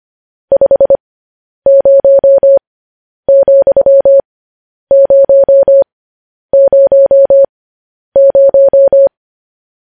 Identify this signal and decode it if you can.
Morse code, heard as a keyed tone, about 25 wpm, spaced as for 10 wpm, 550 Hz. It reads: 50,000